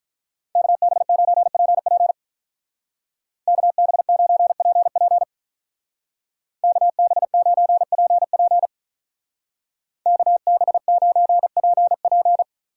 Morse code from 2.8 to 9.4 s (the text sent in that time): KB9PP KB9PP